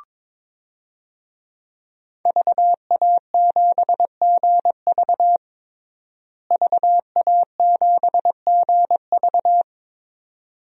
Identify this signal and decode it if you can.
Morse code, heard as a keyed tone, about 22 wpm, 700 Hz. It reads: VA7GV VA7GV
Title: DXCC-100 Call Signs - 5 Characters - Repeated 2x - 22wpm